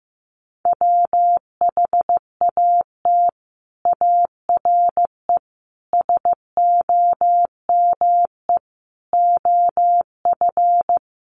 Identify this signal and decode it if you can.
Morse code, heard as a keyed tone, about 15 wpm, 700 Hz. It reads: WHAT ARE SOME OF
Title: Sets of 5 Words Encore 15wpm